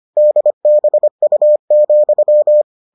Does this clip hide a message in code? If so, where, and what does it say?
Yes: Morse code from 0.2 to 2.6 s, heard as a keyed tone, about 25 wpm, 600 Hz. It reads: DBU,